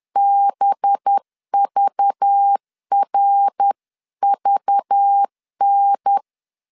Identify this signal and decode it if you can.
Morse code, heard as a keyed tone, about 10 wpm, 800 Hz. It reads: BVRVN